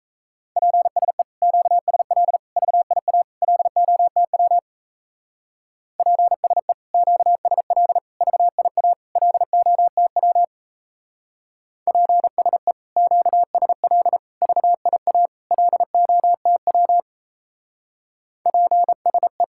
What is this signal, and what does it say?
Morse code, heard as a keyed tone, about 34 wpm, 700 Hz. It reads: PSE QSL VIA LOTW PSE QSL VIA LOTW PSE QSL VIA LOTW PSE
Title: Speed-Racing Common QSO Phrases - 28wpm